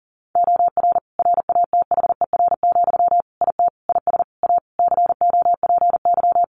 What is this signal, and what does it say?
Morse code, heard as a keyed tone, about 37 wpm, 700 Hz. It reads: OR RATHER, IT IS A COPY
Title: Sentences from Top 1400 Words 40wpm